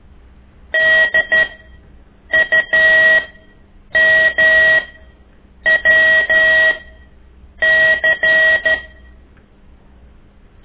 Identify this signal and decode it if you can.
Morse code, heard as a keyed tone, about 12 wpm, spaced as for 7 wpm, 1900 Hz. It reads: DUMWC
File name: ik1weg2 / CORSI CW